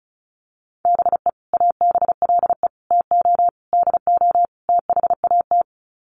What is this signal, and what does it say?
Morse code, heard as a keyed tone, about 34 wpm, 700 Hz. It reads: BE ABLE TO DO THAT
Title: Sets of 5 Words Encore2 35wpm